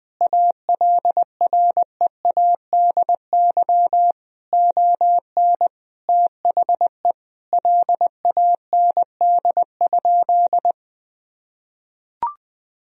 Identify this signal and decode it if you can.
Morse code, heard as a keyed tone, about 21 wpm, 700 Hz. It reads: ALREADY ON THE LAND?
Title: Sentences from Top 1000 Words - Review Part 1 - 20wpm